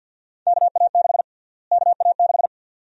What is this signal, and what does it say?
Morse code, heard as a keyed tone, about 46 wpm, 700 Hz. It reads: KA6 KA6
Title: USA Call Sign Prefixes - Repeated 2x - 50wpm